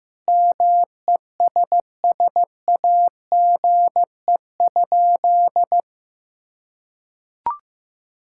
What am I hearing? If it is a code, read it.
Morse code, heard as a keyed tone, about 15 wpm, 700 Hz. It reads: MESSAGE?
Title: Sentences from Top 1600 Words 15wpm